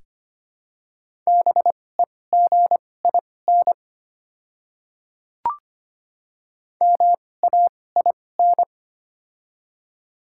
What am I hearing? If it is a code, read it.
Morse code, heard as a keyed tone, about 25 wpm, spaced as for 18 wpm, 700 Hz. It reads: BEGIN MAIN
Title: 2x Character Spacing - Top 300 Words - 25wpm